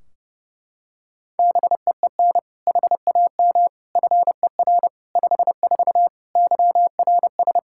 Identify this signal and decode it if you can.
Morse code, heard as a keyed tone, about 30 wpm, 700 Hz. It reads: BEEN HAM FER 54 YRS
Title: Practice QSO - Exchange 3 - More Info - 30wpm